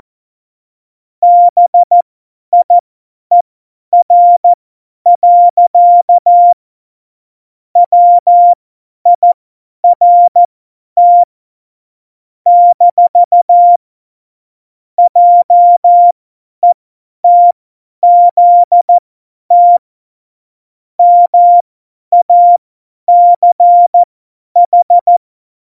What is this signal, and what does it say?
Morse code, heard as a keyed tone, about 14 wpm, spaced as for 10 wpm, 700 Hz. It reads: BIER. WIRT - JETZT MACH
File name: Morsekurs / Witze / Witz 29. Dezember 2020, 50 F70